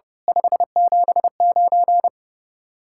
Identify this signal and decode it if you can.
Morse code, heard as a keyed tone, about 33 wpm, 700 Hz. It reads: 579
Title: Speed-Racing Common QSO Phrases - 30wpm